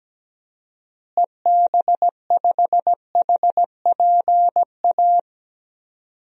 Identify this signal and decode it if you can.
Morse code, heard as a keyed tone, about 17 wpm, 700 Hz. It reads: EB5HPA
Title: DXCC-100 Call Signs - 6 Characters - Repeated 3x - 17wpm